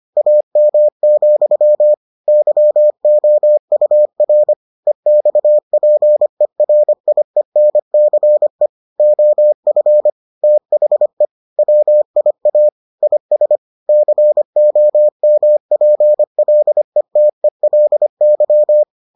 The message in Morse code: AM, YOUR EXPERIENCE OF THE WIA IS COMPLETELY

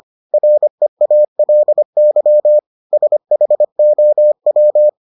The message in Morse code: REALY SHOW

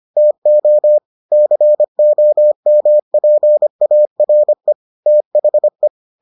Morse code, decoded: TO COMPARE THE

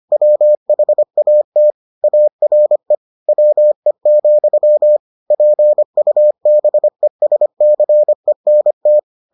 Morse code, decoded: WHAT ARE WE, PUBESCENT

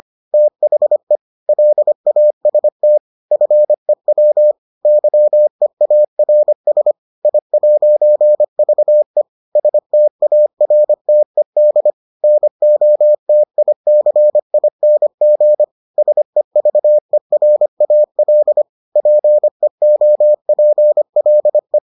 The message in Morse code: THE LAST FEW YEARS I'VE STARTED NOTICING SEVERAL PEOPLE

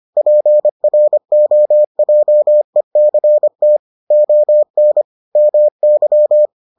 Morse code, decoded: PROJECT ON MY

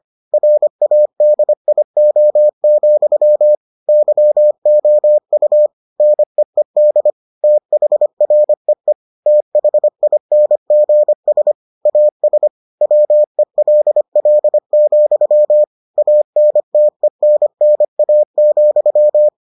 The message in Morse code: RADIO, YOU NEED THREE THINGS AS WELL, ANTENNA,

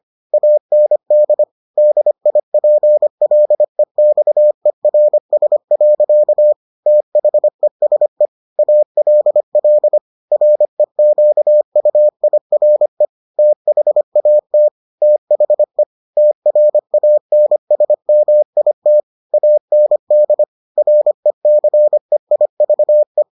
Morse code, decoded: AND DIPLEXERS. THESE ALL REQUIRE THAT THE TRANSMIT AND RECEIVE

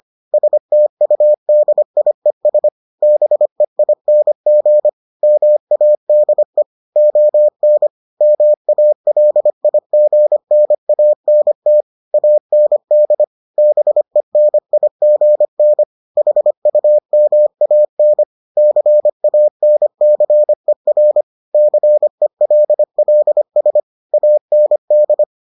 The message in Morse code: STUDIES BEING MADE ON MALIGNANT AND BENIGN HUMAN CANCER CELLS AND

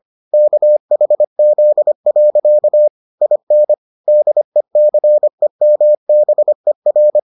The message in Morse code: KHZ. IN DECEMBER

